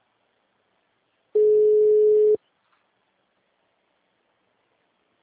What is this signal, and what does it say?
Ringback tone (Europe, 425 Hz).